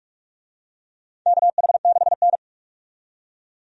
Morse code, heard as a keyed tone, about 44 wpm, 700 Hz. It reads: KH6N
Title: DXCC-100 Call Signs - 4 Characters - 45wpm